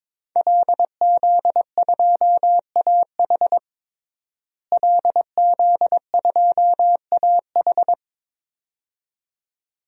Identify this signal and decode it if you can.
Morse code, heard as a keyed tone, about 22 wpm, 700 Hz. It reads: LZ2AH LZ2AH
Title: DXCC-100 Call Signs - 5 Characters - Repeated 2x - 22wpm